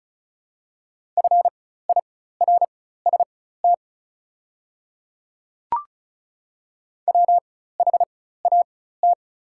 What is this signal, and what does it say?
Morse code, heard as a keyed tone, about 36 wpm, spaced as for 16 wpm, 700 Hz. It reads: FIRST WHAT
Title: 4x Character Spacing - Top 100 Words - 35wpm